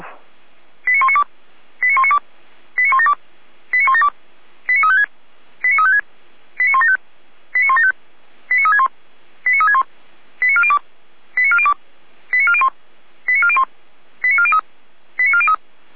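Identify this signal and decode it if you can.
Five-tone selective call (ZVEI), heard as quick runs of five tones, tones of about 70 ms. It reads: 89192 89192 89172 89172 89356 89356 89175 89175 89361 89361 89402 89402 89401 89401 89403 89403
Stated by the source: ZVEI five-tone call received by SDR, served by feuerwehr-orlamuende.org fire brigade.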